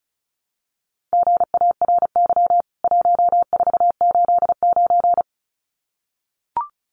Morse code, heard as a keyed tone, about 34 wpm, 700 Hz. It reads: GARY 1489
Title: CWT Exchange 35wpm